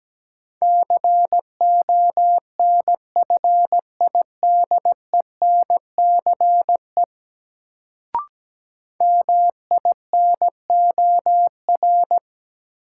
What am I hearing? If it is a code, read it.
Morse code, heard as a keyed tone, about 17 wpm, 700 Hz. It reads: CONFIDENCE MINOR
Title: Sentences from Top 1500 Words - New Word Drill Alt - 17wpm